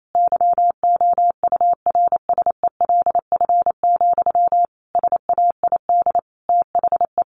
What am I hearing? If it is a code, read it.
Morse code, heard as a keyed tone, about 28 wpm, 700 Hz. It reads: YOURSELF, SAID THE
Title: Sentences from Top 1100 Words 28wpm